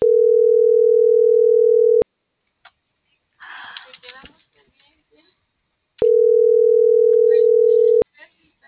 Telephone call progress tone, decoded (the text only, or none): ringback tone